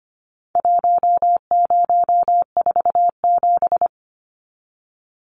Morse code, heard as a keyed tone, about 25 wpm, 700 Hz. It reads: 1047
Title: Numbers - Contest Style - 4-Digit 25wpm